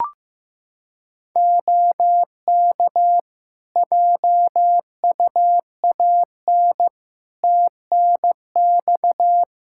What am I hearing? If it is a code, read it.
Morse code, heard as a keyed tone, about 15 wpm, 700 Hz. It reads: OK JUAN TNX
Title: K1USN SST QSO Practice 15wpm